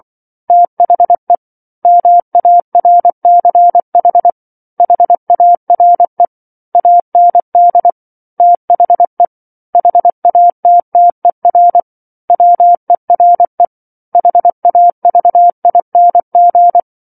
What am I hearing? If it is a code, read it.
Morse code, heard as a keyed tone, about 24 wpm, 700 Hz. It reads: THE MARCH HARE AND THE HATTER WERE HAVING
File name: 07-Chapter-07 24 WPM